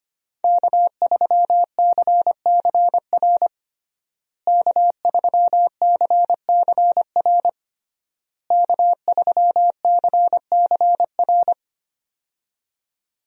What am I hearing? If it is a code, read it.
Morse code, heard as a keyed tone, about 25 wpm, 700 Hz. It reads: K3CCR K3CCR K3CCR
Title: USA Call Signs - 5 Characters - Repeated 3x - 25wpm